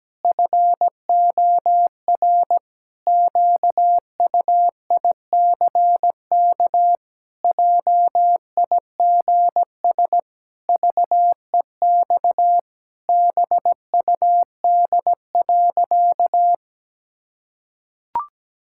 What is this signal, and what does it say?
Morse code, heard as a keyed tone, about 17 wpm, 700 Hz. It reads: FOR QUICK JIGS VEX BUD.
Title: Pangram Challenge v2 - 17wpm